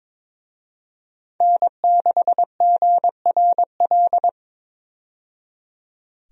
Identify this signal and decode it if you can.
Morse code, heard as a keyed tone, about 22 wpm, 700 Hz. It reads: N6GRL
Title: USA Call Signs - 5 Characters - 22wpm